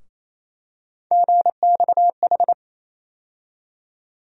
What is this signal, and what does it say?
Morse code, heard as a keyed tone, about 28 wpm, 700 Hz. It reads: GXH
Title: USA Call Sign Postfixes - 3 Letters 28wpm